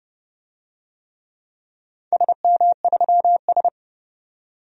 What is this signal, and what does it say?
Morse code, heard as a keyed tone, about 30 wpm, 700 Hz. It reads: SM3S
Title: DXCC-100 Call Signs - 4 Characters - 30wpm